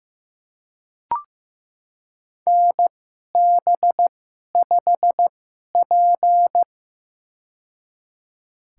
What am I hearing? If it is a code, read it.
Morse code, heard as a keyed tone, about 15 wpm, spaced as for 11 wpm, 700 Hz. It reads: NB5P
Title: USA Call Signs - 4 Characters - 2x Character Spacing - 15wpm